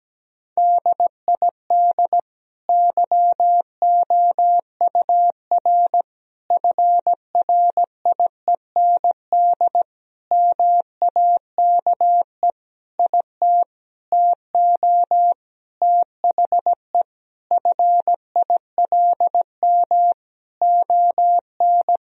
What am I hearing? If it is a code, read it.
Morse code, heard as a keyed tone, about 17 wpm, 700 Hz. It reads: DID YOUR FRIEND MAKE IT TO THE FILM ON
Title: Sentences from Top 1000 Words - Review Part 3 - 17wpm